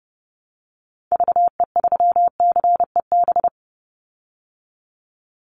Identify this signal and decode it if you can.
Morse code, heard as a keyed tone, about 30 wpm, 700 Hz. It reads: VE3CEB
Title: Call Signs Repeated 2x - 30wpm